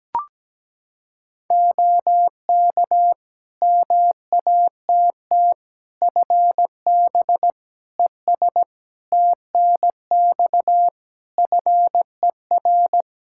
Morse code, heard as a keyed tone, about 17 wpm, 700 Hz. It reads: OK MATT FB ES TNX FER
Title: Practice QSO - Exchange 2 - Info - 17wpm